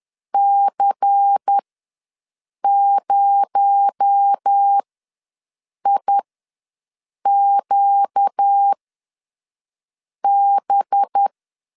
Morse code, heard as a keyed tone, about 10 wpm, spaced as for 5 wpm, 800 Hz. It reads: C0IQB
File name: LEZIONI / 9 28 ws 10wpm 4124